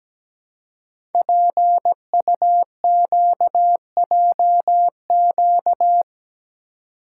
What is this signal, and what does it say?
Morse code, heard as a keyed tone, about 17 wpm, 700 Hz. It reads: PUQJQ
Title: Code Groups - 5 Letters - 17wpm